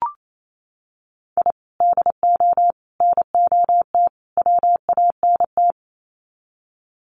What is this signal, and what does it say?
Morse code, heard as a keyed tone, about 28 wpm, 700 Hz. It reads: I DO NOT WANT
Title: Sets of 4 Words 28wpm